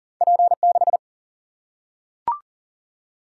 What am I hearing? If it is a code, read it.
Morse code, heard as a keyed tone, about 40 wpm, 700 Hz. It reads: P6
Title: Single-Letter Single-Number 40wpm